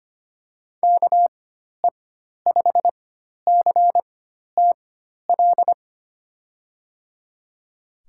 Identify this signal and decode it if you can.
Morse code, heard as a keyed tone, about 25 wpm, spaced as for 12 wpm, 700 Hz. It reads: KE5CTL
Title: USA Call Signs - 6 Characters - 4x Character Spacing - 25wpm